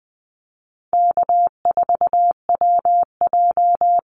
Morse code, heard as a keyed tone, about 20 wpm, 700 Hz. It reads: K4WJ
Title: Call Signs Repeated 2x - 20wpm